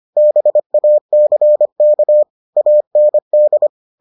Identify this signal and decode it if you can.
Morse code, heard as a keyed tone, about 25 wpm, 600 Hz. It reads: BACK AND